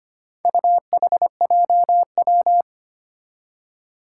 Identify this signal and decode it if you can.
Morse code, heard as a keyed tone, about 25 wpm, 700 Hz. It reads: UHJW
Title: Four Challenging Letter-Number Combinations 25wpm